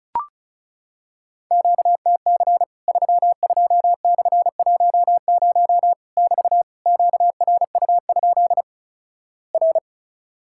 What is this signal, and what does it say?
Morse code, heard as a keyed tone, about 36 wpm, 700 Hz. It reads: QTC 32/10 <BT> QRU? R
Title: Worked All Europe DX Contest - QTC Exchange - 35wpm